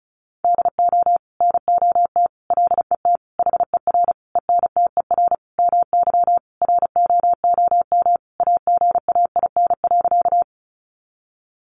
Morse code, heard as a keyed tone, about 34 wpm, 700 Hz. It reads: DO NOT LET HER ENTER MY ROOM AGAIN.
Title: Sentences from Top 1300 Words 35wpm